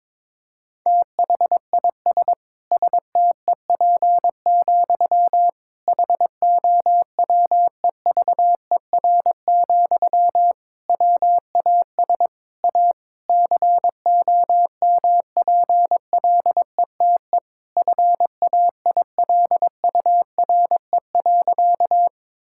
Morse code, heard as a keyed tone, about 22 wpm, 700 Hz. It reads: THIS STEP, HOWEVER, WAS A COMPLETE FAILURE.
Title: Sentences from Top 1000 Words - Review Part 6 - 22wpm